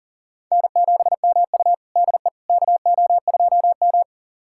Morse code, heard as a keyed tone, about 40 wpm, 700 Hz. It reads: N7MU DE KO2M
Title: USA Callsign DE Callsign - 4 Characters - 40wpm